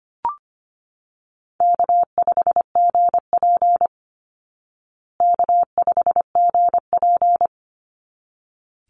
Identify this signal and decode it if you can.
Morse code, heard as a keyed tone, about 25 wpm, 700 Hz. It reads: K5GP K5GP
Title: Easy Call Signs Repeated 2x - 25wpm